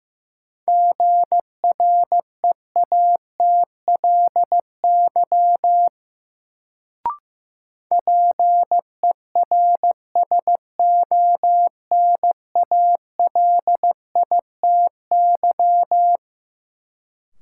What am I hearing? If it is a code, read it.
Morse code, heard as a keyed tone, about 15 wpm, 700 Hz. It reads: GREATLY PERSONALITY
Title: Sentences from Top 1400 Words - New Word Drill Alt - 15wpm